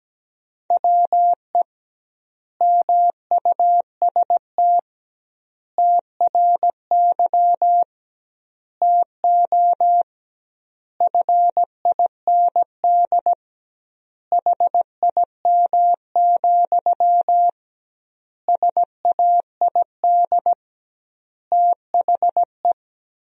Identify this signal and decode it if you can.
Morse code, heard as a keyed tone, about 17 wpm, 700 Hz. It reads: WE MUST TRY TO FIND HIM, SAID THE
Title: 2x Word-Spacing - Sentences from Top 400 Words - 17wpm